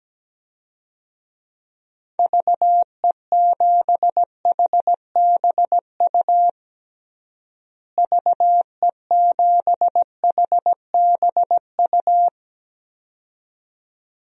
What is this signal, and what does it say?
Morse code, heard as a keyed tone, about 17 wpm, 700 Hz. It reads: VE7HBU VE7HBU
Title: DXCC-100 Call Signs - 6 Characters - Repeated 2x - 17wpm